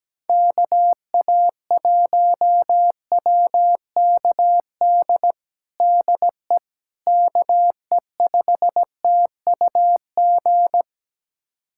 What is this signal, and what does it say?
Morse code, heard as a keyed tone, about 17 wpm, 700 Hz. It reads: KA1WKD DE KE5TUG
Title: USA Callsign DE Callsign - 6 Characters - 17wpm